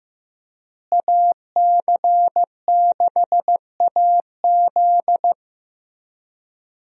Morse code, heard as a keyed tone, about 16 wpm, 700 Hz. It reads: AC6AZ
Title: USA Call Signs - 5 Characters - Repeated 3x - 15wpm